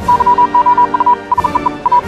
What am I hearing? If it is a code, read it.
Morse code, heard as a keyed tone, about 40 wpm, 980 Hz. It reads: YOU SEE A
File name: morse hymne(1)